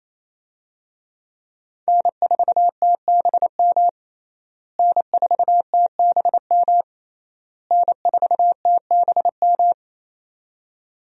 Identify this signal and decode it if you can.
Morse code, heard as a keyed tone, about 27 wpm, 700 Hz. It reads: N4TBM N4TBM N4TBM
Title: USA Call Signs - 5 Characters - Repeated 3x - 28wpm